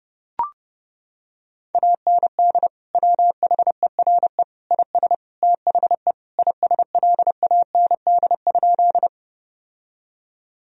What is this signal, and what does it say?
Morse code, heard as a keyed tone, about 30 wpm, 700 Hz. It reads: AND WHERE IS THE ISLAND?
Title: Sentences from Top 1000 Words - Review Part 3 - 30wpm